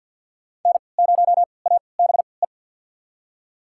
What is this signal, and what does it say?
Morse code, heard as a keyed tone, about 53 wpm, spaced as for 29 wpm, 690 Hz. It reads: N0ABE
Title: USA Call Signs - 5 Characters - 3x Character Spacing - 50wpm